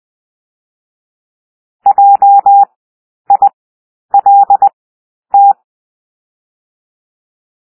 Morse code, heard as a keyed tone, about 20 wpm, 800 Hz. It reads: J I L T
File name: cwmp3 / W1 SOUND WORDS 20 wpm